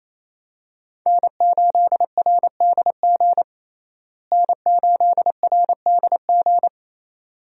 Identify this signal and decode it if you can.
Morse code, heard as a keyed tone, about 28 wpm, 700 Hz. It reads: N8RDG N8RDG
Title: USA Call Signs - 5 Characters - Repeated 2x - 28wpm